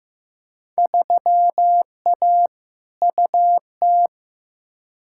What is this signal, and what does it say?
Morse code, heard as a keyed tone, about 15 wpm, 700 Hz. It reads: 3A UT
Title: Field Day Exchanges - 15wpm